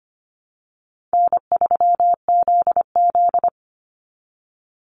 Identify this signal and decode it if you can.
Morse code, heard as a keyed tone, about 25 wpm, 700 Hz. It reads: N3ZZ
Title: Call Signs Repeated 2x - 25wpm